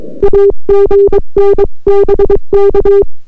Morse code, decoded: AGN BK